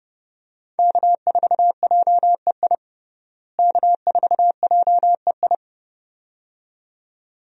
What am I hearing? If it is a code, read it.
Morse code, heard as a keyed tone, about 30 wpm, 700 Hz. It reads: K4JEI K4JEI
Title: USA Call Signs - 5 Characters - Repeated 2x - 30wpm